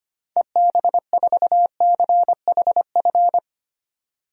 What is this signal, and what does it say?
Morse code, heard as a keyed tone, about 25 wpm, 700 Hz. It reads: EB4CHF
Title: DXCC-100 Call Signs - 6 Characters - 25wpm